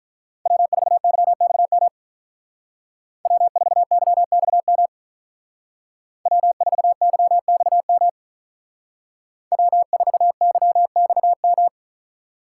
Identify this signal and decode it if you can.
Morse code, heard as a keyed tone, about 43 wpm, 700 Hz. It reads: W4YXM W4YXM W4YXM W4YXM
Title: Speed Racing - USA 1x3 Call Signs 35wpm